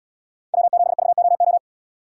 Morse code, heard as a keyed tone, about 77 wpm, 700 Hz. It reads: W7VOO